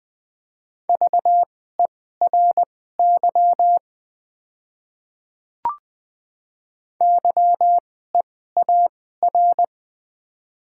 Morse code, heard as a keyed tone, about 20 wpm, spaced as for 14 wpm, 700 Hz. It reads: VERY YEAR
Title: 2x Character Spacing - Top 300 Words - 20wpm